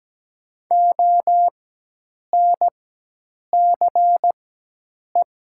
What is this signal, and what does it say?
Morse code, heard as a keyed tone, about 17 wpm, spaced as for 8 wpm, 700 Hz. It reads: ONCE
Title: 4x Character Spacing - Top 500 Words - 17wpm